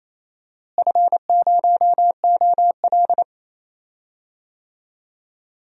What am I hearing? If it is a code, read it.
Morse code, heard as a keyed tone, about 28 wpm, 700 Hz. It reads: F0OL
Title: DXCC-100 Call Signs - 4 Characters - Repeated 2x - 28wpm